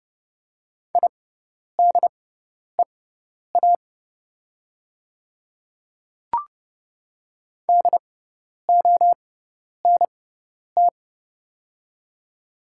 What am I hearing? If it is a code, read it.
Morse code, heard as a keyed tone, about 30 wpm, spaced as for 10 wpm, 700 Hz. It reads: IDEA DONT
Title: 6x Character Spacing - Top 300 Words - 30wpm